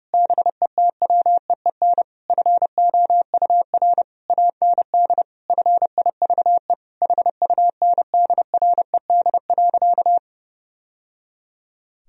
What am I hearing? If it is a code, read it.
Morse code, heard as a keyed tone, about 30 wpm, 700 Hz. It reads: BETWEEN FOUR AND FIVE HUNDRED.
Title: Sentences from Top 1000 Words - Review Part 2 - 30wpm